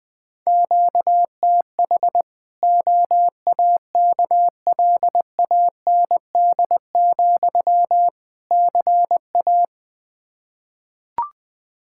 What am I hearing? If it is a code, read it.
Morse code, heard as a keyed tone, about 20 wpm, 700 Hz. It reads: QTH OAKLAND, CA
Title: K1USN SST QSO Practice 20wpm